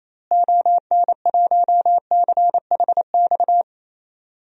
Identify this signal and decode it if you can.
Morse code, heard as a keyed tone, about 28 wpm, 700 Hz. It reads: ON1CHX
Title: DXCC-100 Call Signs - 6 Characters - Repeated 2x - 28wpm